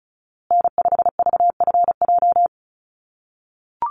Morse code, heard as a keyed tone, about 34 wpm, 700 Hz. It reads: N5VFJ
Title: USA Call Signs Repeated 3x 35wpm